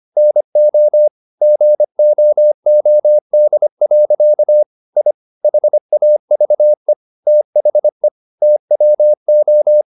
Morse code, heard as a keyed tone, about 25 wpm, 600 Hz. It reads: NO GOOD. I HAVE THE TWO